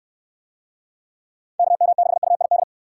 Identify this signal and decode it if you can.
Morse code, heard as a keyed tone, about 64 wpm, 700 Hz. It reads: KM6UID